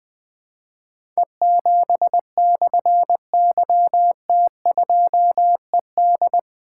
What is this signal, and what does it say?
Morse code, heard as a keyed tone, about 21 wpm, 700 Hz. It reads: E7/YT2ED
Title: Difficult Call Signs 20wpm